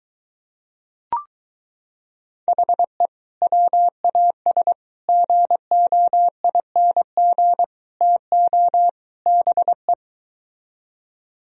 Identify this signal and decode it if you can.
Morse code, heard as a keyed tone, about 23 wpm, 700 Hz. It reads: HE WAS GOING TO BE